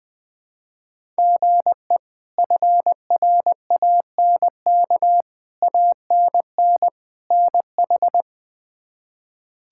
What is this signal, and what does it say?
Morse code, heard as a keyed tone, about 20 wpm, 700 Hz. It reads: GE FRANK ANN NH
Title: K1USN SST QSO Practice - 20wpm